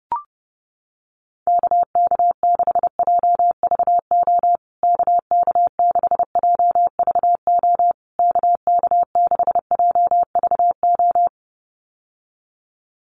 Morse code, heard as a keyed tone, about 30 wpm, 700 Hz. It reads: KK6JVO KK6JVO KK6JVO
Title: USA Call Signs Repeated 3x 30wpm